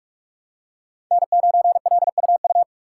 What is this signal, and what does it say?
Morse code, heard as a keyed tone, about 46 wpm, 700 Hz. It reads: N9LUU
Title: USA Call Signs - 5 Characters - 4x Character Spacing - 45wpm